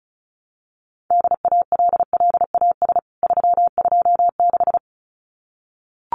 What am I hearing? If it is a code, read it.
Morse code, heard as a keyed tone, about 34 wpm, 700 Hz. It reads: DALLAS 326
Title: CWT Exchange 35wpm